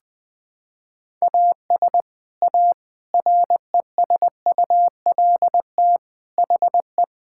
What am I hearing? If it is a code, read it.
Morse code, heard as a keyed tone, about 20 wpm, 700 Hz. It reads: AS A RESULT HE
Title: Sentences with 3-Word Phrases - 20wpm